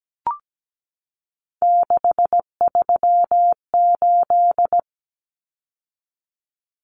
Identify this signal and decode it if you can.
Morse code, heard as a keyed tone, about 17 wpm, 700 Hz. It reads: 638
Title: Numbers - 3-Digit 17wpm